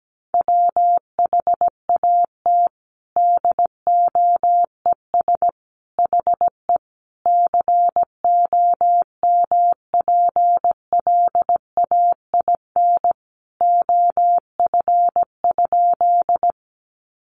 Morse code, heard as a keyed tone, about 17 wpm, 700 Hz. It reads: WHAT DOES HE COMPLAIN OF?
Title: Sentences from Top 1300 Words 17wpm